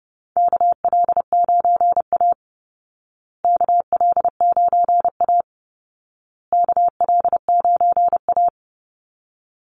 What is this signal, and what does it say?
Morse code, heard as a keyed tone, about 30 wpm, 700 Hz. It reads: KL9A KL9A KL9A
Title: Call Signs Repeated 2x - 30wpm